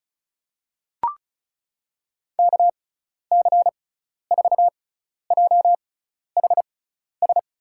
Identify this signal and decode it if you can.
Morse code, heard as a keyed tone, about 36 wpm, spaced as for 12 wpm, 690 Hz. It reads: KC4JHS
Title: USA Call Signs - 6 Characters - 6x Character Spacing - 35wpm